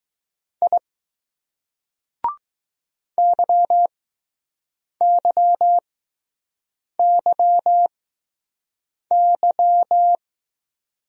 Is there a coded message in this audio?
Yes: Morse code from 0.6 to 10.2 s, heard as a keyed tone, about 20 wpm, spaced as for 6 wpm, 700 Hz. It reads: I YYYY